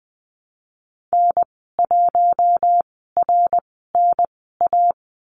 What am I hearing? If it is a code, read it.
Morse code, heard as a keyed tone, about 20 wpm, spaced as for 15 wpm, 700 Hz. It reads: N1RNA
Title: 069 Lesson 12g - Letter d - Call Signs 20wpm